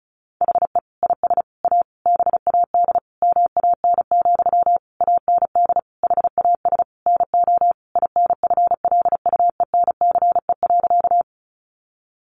Morse code, heard as a keyed tone, about 34 wpm, 700 Hz. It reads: HE IS A BAD MAN, AND HAS NO INFLUENCE.
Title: Sentences from Top 1100 Words 35wpm